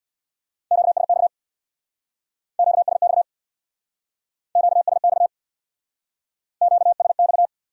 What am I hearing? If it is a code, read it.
Morse code, heard as a keyed tone, about 61 wpm, 700 Hz. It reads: QSX QSX QSX QSX
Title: Speed-Racing Q-Codes - 50wpm